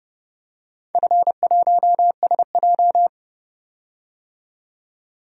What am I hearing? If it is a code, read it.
Morse code, heard as a keyed tone, about 30 wpm, 700 Hz. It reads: F1SJ
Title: DXCC-100 Call Signs - 4 Characters - 30wpm